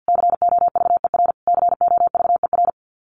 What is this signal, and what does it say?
Morse code, heard as a keyed tone, about 50 wpm, 700 Hz. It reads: COVER COVER
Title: Top 500 Words Repeated 50wpm